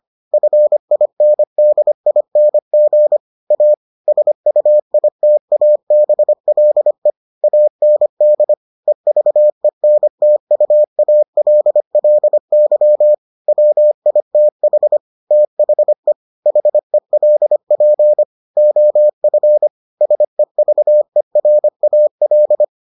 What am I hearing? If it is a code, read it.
Morse code, heard as a keyed tone, about 25 wpm, 600 Hz. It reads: FINDING A SUITABLE AND EVENTUALLY WITH THE HELP OF SEVERAL